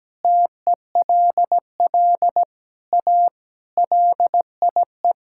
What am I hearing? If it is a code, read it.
Morse code, heard as a keyed tone, about 17 wpm, 700 Hz. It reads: TELL A LIE
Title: Sentences from Top 1600 Words 17wpm